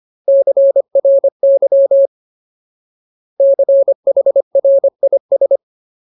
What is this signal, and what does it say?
Morse code, heard as a keyed tone, about 25 wpm, 550 Hz. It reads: CRY CHRIS
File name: Ultimate Word List / 25WPM Ultimate